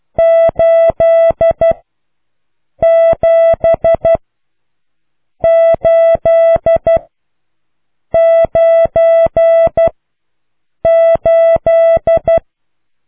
Morse code, heard as a keyed tone, about 12 wpm, spaced as for 6 wpm, 660 Hz. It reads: 87898